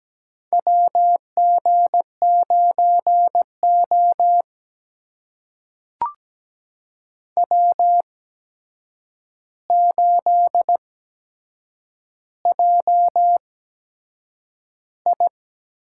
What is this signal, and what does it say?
Morse code, heard as a keyed tone, about 17 wpm, spaced as for 7 wpm, 700 Hz. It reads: WG9O W8JI
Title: USA Call Signs - 4 Characters - 8x Character Spacing - 17wpm